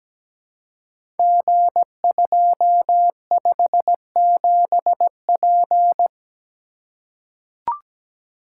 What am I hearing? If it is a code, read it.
Morse code, heard as a keyed tone, about 17 wpm, 700 Hz. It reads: G257P
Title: Code Groups - 5 Letter:Numbers - 17wpm